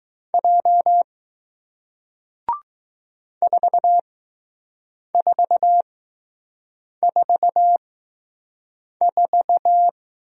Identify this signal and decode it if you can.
Morse code, heard as a keyed tone, about 20 wpm, spaced as for 6 wpm, 700 Hz. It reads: J 4444